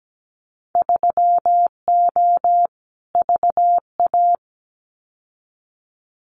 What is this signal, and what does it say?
Morse code, heard as a keyed tone, about 17 wpm, 700 Hz. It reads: 3O VA
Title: Winter Field Day Exchanges 17wpm